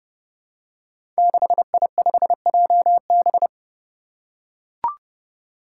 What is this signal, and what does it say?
Morse code, heard as a keyed tone, about 28 wpm, 700 Hz. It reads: 6I5JB
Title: Five Challenging Letter-Number Combinations 30wpm